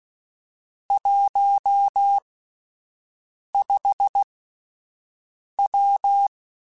Morse code, heard as a keyed tone, about 16 wpm, spaced as for 5 wpm, 800 Hz. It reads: 15W
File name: (spaced) / QSOs Spaced-mp3s / WD0BGP 1 @10 sp